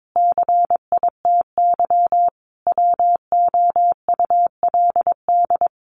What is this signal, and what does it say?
Morse code, heard as a keyed tone, about 22 wpm, 700 Hz. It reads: CITY WOULD
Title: Sentences from Top 1100 Words 22wpm